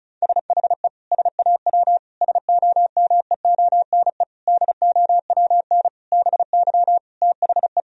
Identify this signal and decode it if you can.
Morse code, heard as a keyed tone, about 36 wpm, 700 Hz. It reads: SHE SAW SOMEONE DOWN BY THE
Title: Sentences from Top 1000 Words - Review Part 4 - 35wpm